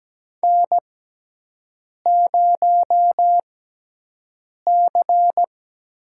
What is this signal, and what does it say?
Morse code, heard as a keyed tone, about 17 wpm, spaced as for 6 wpm, 700 Hz. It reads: N0C
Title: USA Call Signs - 6x Character Spacing - 17wpm